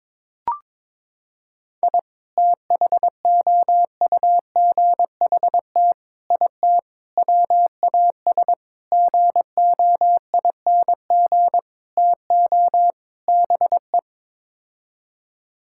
Morse code, heard as a keyed tone, about 22 wpm, 700 Hz. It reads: I THOUGHT IT WAS GOING TO BE
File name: practice / 22wpm / Sets of 7 Words - 22wpm